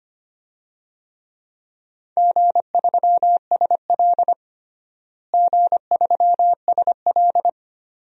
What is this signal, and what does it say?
Morse code, heard as a keyed tone, about 25 wpm, 700 Hz. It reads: G3SL G3SL
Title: DXCC-100 Call Signs - 4 Characters - Repeated 2x - 25wpm